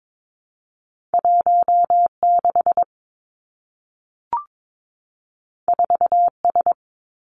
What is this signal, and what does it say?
Morse code, heard as a keyed tone, about 22 wpm, 700 Hz. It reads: 1 6 4 S
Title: Two Challenging Letter-Number Combinations 22wpm